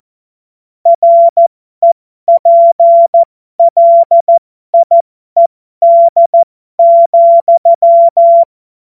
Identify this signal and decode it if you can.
Morse code, heard as a keyed tone, about 14 wpm, spaced as for 12 wpm, 680 Hz. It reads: REPLIED,
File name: Morsekurs / Witze